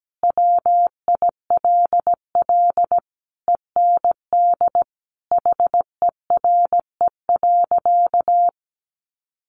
Morse code, heard as a keyed tone, about 17 wpm, 700 Hz. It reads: WILL END HERE.
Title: Sentences from Top 1200 Words 17wpm